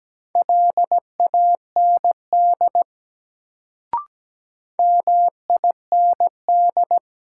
Morse code, heard as a keyed tone, about 17 wpm, 700 Hz. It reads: LAND MIND
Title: Sentences from Top 500 Words - New Word Drill - 17wpm